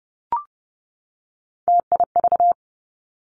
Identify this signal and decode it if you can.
Morse code, heard as a keyed tone, about 30 wpm, 700 Hz. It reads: TIV
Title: Three Challenging Letter-Number Combinations 30wpm